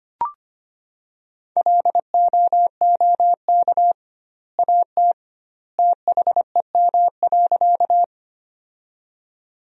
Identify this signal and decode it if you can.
Morse code, heard as a keyed tone, about 25 wpm, 700 Hz. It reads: LOOK AT THEM.
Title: 2x Word-Spacing - Sentences from Top 100 Words - 25wpm